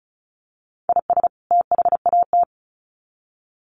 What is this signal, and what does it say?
Morse code, heard as a keyed tone, about 34 wpm, 700 Hz. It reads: IS THAT
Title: Sets of 2 Words 35wpm